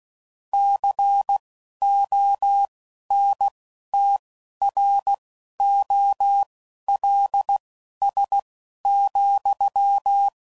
Morse code, heard as a keyed tone, about 16 wpm, spaced as for 11 wpm, 800 Hz. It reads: CONTROLS,
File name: CW Code Groups @10WPM/Technical Subjects